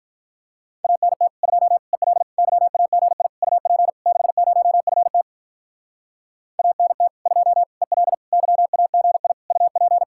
Morse code, heard as a keyed tone, about 44 wpm, 700 Hz. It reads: ANT 2 EL YAGI UP 60FT ANT 2 EL YAGI UP